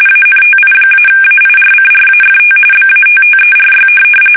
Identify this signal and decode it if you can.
RTTY, about 45.45 baud. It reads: ANTI-AIRCRAFT BATTERY
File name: fsk850 / USS-Massachusetts